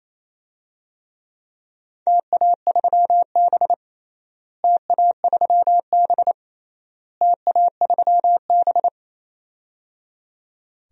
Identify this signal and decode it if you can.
Morse code, heard as a keyed tone, about 28 wpm, 700 Hz. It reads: TA3B TA3B TA3B
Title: DXCC-100 Call Signs - 4 Characters - Repeated 3x - 28wpm